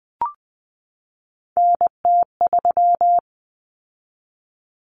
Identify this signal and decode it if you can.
Morse code, heard as a keyed tone, about 20 wpm, 700 Hz. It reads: NT3
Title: Common Cut Numbers - 3-Digit 20wpm